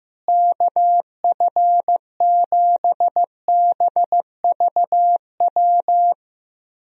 Morse code, heard as a keyed tone, about 14 wpm, 700 Hz. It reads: KF7BVW
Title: USA Call Signs - 6 Characters - Repeated 3x - 15wpm